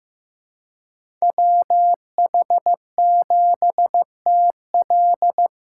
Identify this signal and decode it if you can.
Morse code, heard as a keyed tone, about 15 wpm, 700 Hz. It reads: WH7TL
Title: USA Callsign DE Callsign - 5 Characters - 15wpm